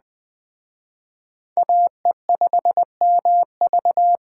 Morse code, heard as a keyed tone, about 20 wpm, 700 Hz. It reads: AE5MV
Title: Speed Racing - USA 2x2 Call Signs 17wpm